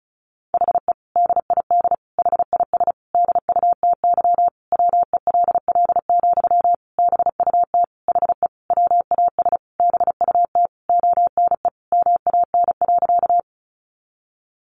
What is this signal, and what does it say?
Morse code, heard as a keyed tone, about 34 wpm, 700 Hz. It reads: HE DID HIS DUTY WELL, BUT HE WAS BUT ONE MAN.
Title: Sentences from Top 1500 Words 35wpm